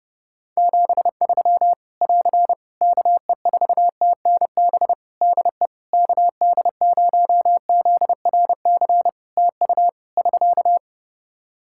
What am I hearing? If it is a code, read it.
Morse code, heard as a keyed tone, about 30 wpm, 700 Hz. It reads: 73 <AR> KE4TNB DE KD0ZRC TU <SK>
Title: K1USN SST QSO Practice 30wpm